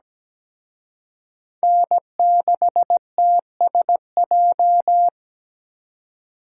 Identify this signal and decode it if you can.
Morse code, heard as a keyed tone, about 20 wpm, 700 Hz. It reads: N6TSJ
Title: Speed Racing - USA 1x3 Call Signs 17wpm